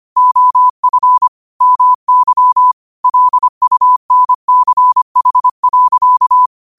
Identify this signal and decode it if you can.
Morse code, heard as a keyed tone, about 25 wpm, 1000 Hz. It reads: OF MY LUNCH.
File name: qotdfiles / 25wpm